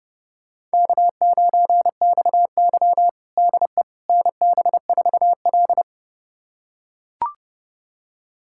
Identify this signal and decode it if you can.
Morse code, heard as a keyed tone, about 30 wpm, 700 Hz. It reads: K9XY DE NB4L
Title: USA Callsign DE Callsign - 4 Characters - 30wpm